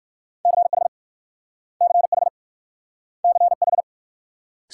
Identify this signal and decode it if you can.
Morse code, heard as a keyed tone, about 54 wpm, 690 Hz. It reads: CH CH CH
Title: Speed-Racing Multi-letter Phonemes - 45wpm